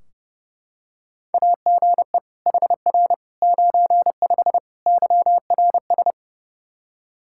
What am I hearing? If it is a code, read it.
Morse code, heard as a keyed tone, about 30 wpm, 700 Hz. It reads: AGE HR 95 YRS
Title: Practice QSO - Exchange 3 - More Info - 30wpm